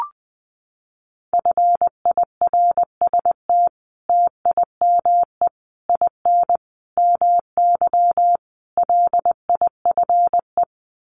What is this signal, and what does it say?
Morse code, heard as a keyed tone, about 20 wpm, 700 Hz. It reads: FIRST TIME IN MY LIFE
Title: Sets of 5 Words Encore2 20wpm